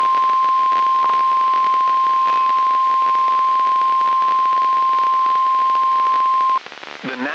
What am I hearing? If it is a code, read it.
Emergency Alert System (SAME) tones (the 1050 Hz attention tone). It is heard from 1.1 to 6.6 s.
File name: eas messages / EAS WXM95 Towanda